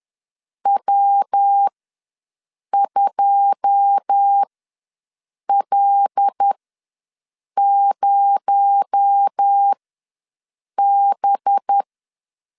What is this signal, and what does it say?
Morse code, heard as a keyed tone, about 10 wpm, spaced as for 6 wpm, 800 Hz. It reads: W2L0B
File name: LEZIONI / 9 2 ws 10wpm 4130